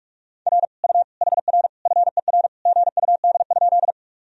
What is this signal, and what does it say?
Morse code, heard as a keyed tone, about 46 wpm, 700 Hz. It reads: R U HR FER GUD?
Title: Sentences from Top 200 Words and Abbreviations 45wpm